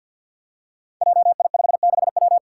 Morse code, heard as a keyed tone, about 51 wpm, 700 Hz. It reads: JI56W
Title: Five Challenging Letter-Number Combinations 50wpm